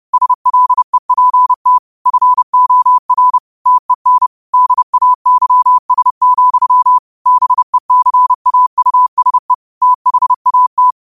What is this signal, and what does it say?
Morse code, heard as a keyed tone, about 30 wpm, 1000 Hz. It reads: SLEPT FOR TEN DAYS, BECAUSE THAT